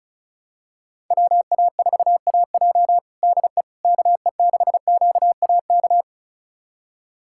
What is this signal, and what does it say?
Morse code, heard as a keyed tone, about 34 wpm, 700 Hz. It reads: WA4AJ DE KE6QAK
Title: USA Callsign DE Callsign 35wpm